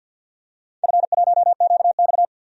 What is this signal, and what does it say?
Morse code, heard as a keyed tone, about 48 wpm, 700 Hz. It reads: F1QX
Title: DXCC-100 Call Signs - 4 Characters - 50wpm